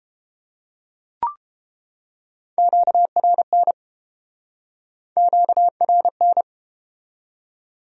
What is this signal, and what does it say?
Morse code, heard as a keyed tone, about 28 wpm, 700 Hz. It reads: QRN QRN